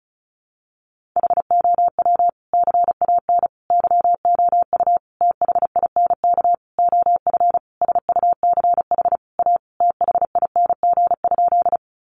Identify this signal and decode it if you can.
Morse code, heard as a keyed tone, about 34 wpm, 700 Hz. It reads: HOW CAN YOU THINK OF SUCH A THING?
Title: Sentences from Top 200 Words 35wpm